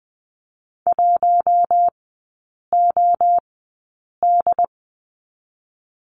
Morse code, heard as a keyed tone, about 20 wpm, 700 Hz. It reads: 1 O D
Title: 114 Lesson 19e - Letter w - Three Characters 20wpm